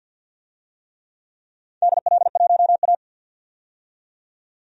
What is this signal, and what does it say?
Morse code, heard as a keyed tone, about 47 wpm, 700 Hz. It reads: DL1A
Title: DXCC-100 Call Signs - 4 Characters - 50wpm